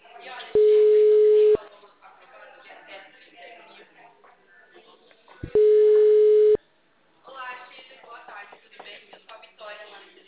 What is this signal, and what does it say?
Ringback tone (Europe, 425 Hz), with speech.
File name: out- -2011- - - .233